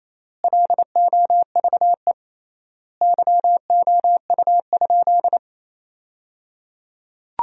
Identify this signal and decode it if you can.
Morse code, heard as a keyed tone, about 28 wpm, 700 Hz. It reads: LOVE YOU?
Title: 3x Word-Spacing - Sentences from Top 400 Words - 28wpm